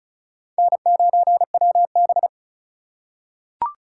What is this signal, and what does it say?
Morse code, heard as a keyed tone, about 34 wpm, 700 Hz. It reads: N9WB
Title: USA Call Signs - 4 Characters - 8x Character Spacing - 35wpm